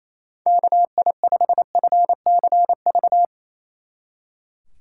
Morse code, heard as a keyed tone, about 34 wpm, 700 Hz. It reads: KI5FCV
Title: Speed Racing - USA 2x3 Call Signs 28wpm